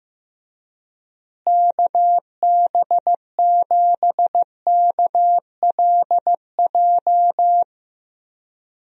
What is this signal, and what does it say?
Morse code, heard as a keyed tone, about 15 wpm, 700 Hz. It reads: KB7KLJ
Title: USA Call Signs - 6 Characters - Repeated 2x - 15wpm